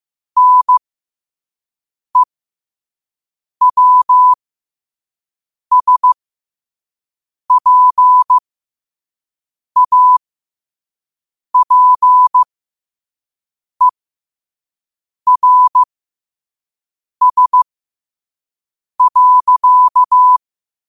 Morse code, heard as a keyed tone, about 15 wpm, spaced as for 5 wpm, 1000 Hz. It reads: NEWSPAPERS.
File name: qotdfiles / 5wpm